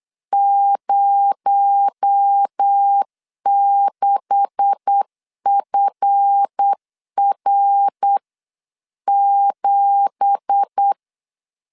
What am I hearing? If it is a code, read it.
Morse code, heard as a keyed tone, about 8 wpm, 800 Hz. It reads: 06FR7
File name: 8wpm